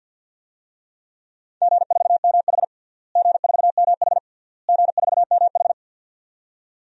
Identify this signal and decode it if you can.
Morse code, heard as a keyed tone, about 52 wpm, 700 Hz. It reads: G4MH G4MH G4MH
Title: DXCC-100 Call Signs - 4 Characters - Repeated 3x - 50wpm